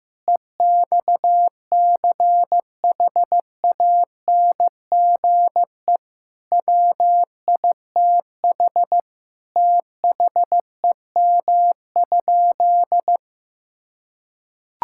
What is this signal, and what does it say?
Morse code, heard as a keyed tone, about 15 wpm, 700 Hz. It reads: EXCHANGE WITH THEM?
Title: Sentences from Top 1000 Words - Review Part 6 - 15wpm